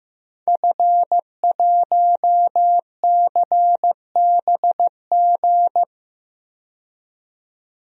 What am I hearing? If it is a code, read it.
Morse code, heard as a keyed tone, about 16 wpm, 700 Hz. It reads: F1CBG
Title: DXCC-100 Call Signs - 5 Characters - 15wpm